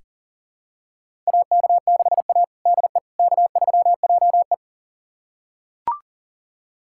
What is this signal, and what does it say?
Morse code, heard as a keyed tone, about 37 wpm, 700 Hz. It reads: AK6A DE K3JE
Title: USA Callsign DE Callsign - 4 Characters - 40wpm